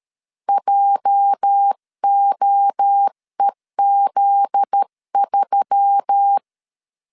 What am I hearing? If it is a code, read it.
Morse code, heard as a keyed tone, about 13 wpm, 800 Hz. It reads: JOEZ3
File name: LEZIONI / 8 YJ45 / 8 YJ45 12wpm 2151